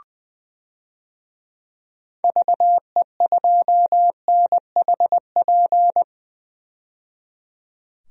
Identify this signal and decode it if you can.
Morse code, heard as a keyed tone, about 20 wpm, 700 Hz. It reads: VE2NHP
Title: DXCC-100 Call Signs - 6 Characters - 20wpm